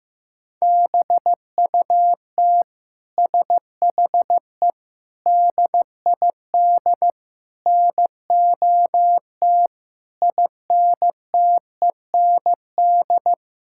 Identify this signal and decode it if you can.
Morse code, heard as a keyed tone, about 15 wpm, 700 Hz. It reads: BUT SHE DID NOT INTEND
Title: Sentences from Top 1600 Words 15wpm